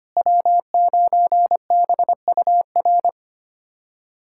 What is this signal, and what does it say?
Morse code, heard as a keyed tone, about 25 wpm, 700 Hz. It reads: W9BUR
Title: USA Callsign DE Callsign 25wpm